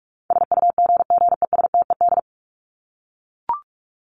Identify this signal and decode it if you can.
Morse code, heard as a keyed tone, about 44 wpm, 700 Hz. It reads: SUGGESTED
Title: ED Suffix - Top 100 Words with Base Words 45wpm